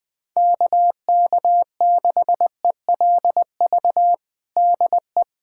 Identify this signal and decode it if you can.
Morse code, heard as a keyed tone, about 20 wpm, 700 Hz. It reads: KK6ELV DE
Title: USA Callsign DE Callsign - 6 Characters - 20wpm